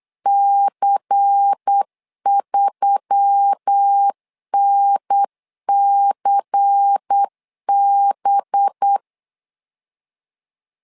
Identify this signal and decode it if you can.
Morse code, heard as a keyed tone, about 8 wpm, 800 Hz. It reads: C3NCB
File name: LEZIONI / 5 LBVC / 5 LBVC 8wpm 2935